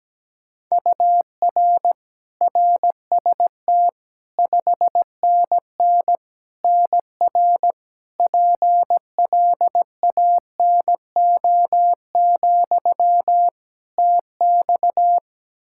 Morse code, heard as a keyed tone, about 17 wpm, 700 Hz. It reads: UR RST 5NN NR PLANO, TX
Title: Practice QSO - Exchange 1 - Beginning - 17wpm